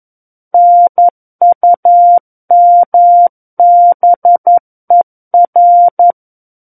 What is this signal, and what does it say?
Morse code, heard as a keyed tone, about 11 wpm, 700 Hz. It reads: NUMBER
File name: 11 WPM